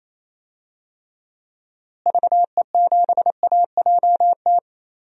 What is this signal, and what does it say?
Morse code, heard as a keyed tone, about 28 wpm, 700 Hz. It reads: VE7AJT
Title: DXCC-100 Call Signs - 6 Characters - Repeated 2x - 28wpm